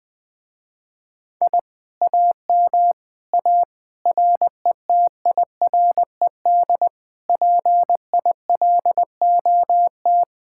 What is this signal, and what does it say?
Morse code, heard as a keyed tone, about 20 wpm, 700 Hz. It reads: I AM A RETIRED PILOT
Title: Speed-Racing Common QSO Phrases Encore - 20wpm